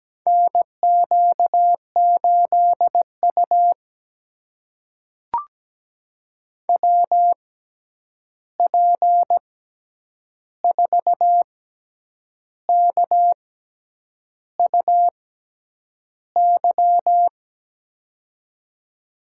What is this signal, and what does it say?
Morse code, heard as a keyed tone, about 17 wpm, spaced as for 8 wpm, 700 Hz. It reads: NQ8U WP4KUY
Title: USA Call Signs - 6x Character Spacing - 17wpm